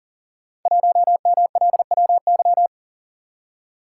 Morse code, heard as a keyed tone, about 43 wpm, 700 Hz. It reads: 1MLWY
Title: Code Groups - 5 Letter:Numbers - 40wpm